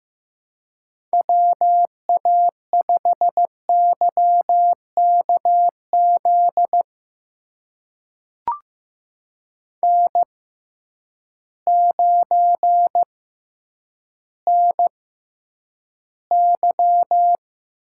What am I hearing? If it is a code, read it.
Morse code, heard as a keyed tone, about 15 wpm, spaced as for 9 wpm, 700 Hz. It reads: WA5YKZ N9NY
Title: USA Call Signs - 6x Character Spacing - 15wpm